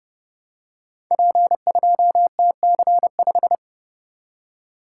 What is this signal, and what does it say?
Morse code, heard as a keyed tone, about 30 wpm, 700 Hz. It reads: P2TC5
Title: Code Groups - 5 Letter:Numbers - 30wpm